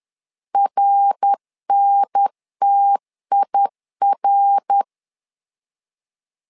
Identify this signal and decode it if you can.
Morse code, heard as a keyed tone, about 10 wpm, 800 Hz. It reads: RNTIR
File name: LEZIONI / 4 Q1S 10wpm 2133